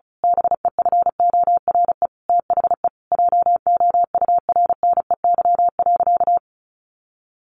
Morse code, heard as a keyed tone, about 34 wpm, 700 Hz. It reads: BEFORE THE JOURNEY.
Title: A Sound Of Thunder 35wpm